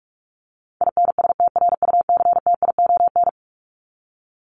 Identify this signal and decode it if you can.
Morse code, heard as a keyed tone, about 44 wpm, 700 Hz. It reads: INSTRUCTION
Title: Top 1751-2000 Words 45wpm